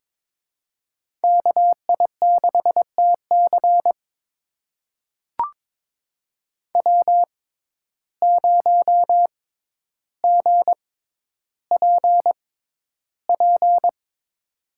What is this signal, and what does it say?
Morse code, heard as a keyed tone, about 22 wpm, spaced as for 11 wpm, 700 Hz. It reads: KI6TC W0GPP
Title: USA Call Signs - 5 Characters - 6x Character Spacing - 22wpm